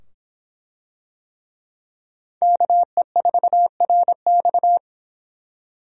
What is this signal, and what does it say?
Morse code, heard as a keyed tone, about 25 wpm, 700 Hz. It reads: KE4RX